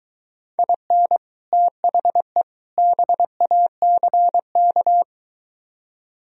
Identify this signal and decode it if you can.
Morse code, heard as a keyed tone, about 22 wpm, 700 Hz. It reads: IN THE BACK